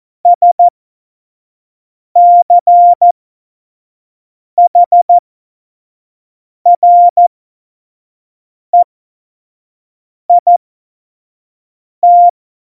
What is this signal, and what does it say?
Morse code, heard as a keyed tone, about 14 wpm, spaced as for 5 wpm, 700 Hz. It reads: SCHREIT
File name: Morsekurs / Witze / Witz 10. September 2019, 25 F70